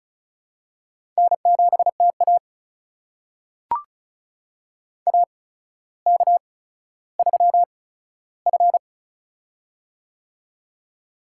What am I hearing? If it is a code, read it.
Morse code, heard as a keyed tone, about 36 wpm, spaced as for 15 wpm, 700 Hz. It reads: N7TA AK3F
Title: USA Call Signs - 4 Characters - 8x Character Spacing - 35wpm